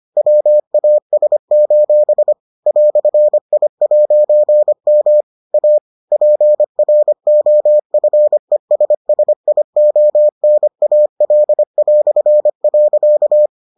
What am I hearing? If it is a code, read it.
Morse code, heard as a keyed tone, about 25 wpm, 600 Hz. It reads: WAS: "I'M A PROFESSIONAL".